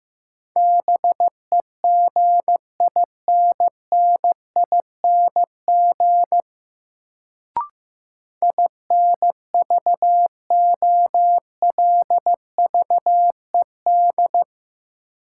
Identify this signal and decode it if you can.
Morse code, heard as a keyed tone, about 15 wpm, 700 Hz. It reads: BEGINNING INVOLVED
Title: Sentences from Top 800 Words - New Word Drill - 15wpm